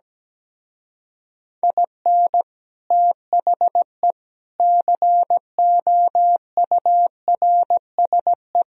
Morse code, heard as a keyed tone, about 17 wpm, 700 Hz. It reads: IN THE COURSE
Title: Speed-Racing - Sets of 5 Words 17wpm